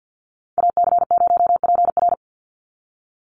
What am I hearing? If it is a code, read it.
Morse code, heard as a keyed tone, about 50 wpm, 700 Hz. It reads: AC0PR
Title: NA SOTA Activator Callsigns 50wpm